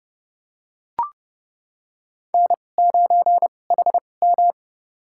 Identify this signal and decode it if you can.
Morse code, heard as a keyed tone, about 30 wpm, spaced as for 22 wpm, 700 Hz. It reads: N9HM
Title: USA Call Signs - 4 Characters - 2x Character Spacing - 30wpm